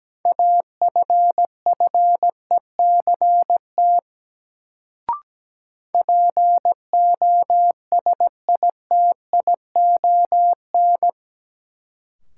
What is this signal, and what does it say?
Morse code, heard as a keyed tone, about 17 wpm, 700 Hz. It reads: AFFECT POSITION
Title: Sentences from Top 900 Words - New Word Drill - 17wpm